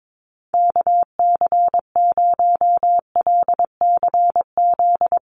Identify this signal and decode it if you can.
Morse code, heard as a keyed tone, about 22 wpm, 700 Hz. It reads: KC0LCZ
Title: USA Call Signs 22wpm